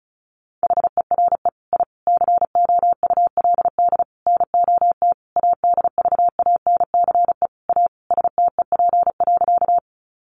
Morse code, heard as a keyed tone, about 34 wpm, 700 Hz. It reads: HERE I COULD NOT ADVANCE A STEP.
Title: Sentences from Top 1400 Words 35wpm